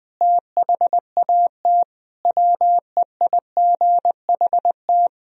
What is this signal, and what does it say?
Morse code, heard as a keyed tone, about 20 wpm, 700 Hz. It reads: THAT WEIGHT
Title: Sentences from Top 1000 Words - Review Part 3 - 20wpm